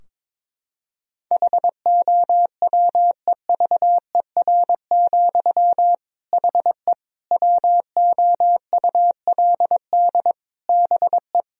Morse code, heard as a keyed tone, about 22 wpm, 700 Hz. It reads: HOWEVER, HE WOULD BE
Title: Sentences from Top 1000 Words - Review Part 2 - 22wpm